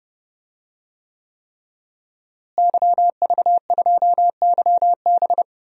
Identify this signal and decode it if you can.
Morse code, heard as a keyed tone, about 29 wpm, 700 Hz. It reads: YV2YB
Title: DXCC-100 Call Signs - 5 Characters - Repeated 3x - 30wpm